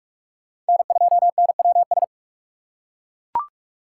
Morse code, heard as a keyed tone, about 39 wpm, 700 Hz. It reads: N2NWS